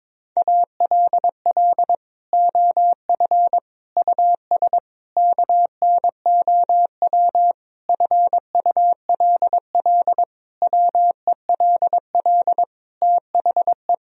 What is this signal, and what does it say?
Morse code, heard as a keyed tone, about 22 wpm, 700 Hz. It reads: ALL OF US KNOW FULL WELL THE
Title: Sentences from Top 1000 Words - Review Part 3 - 22wpm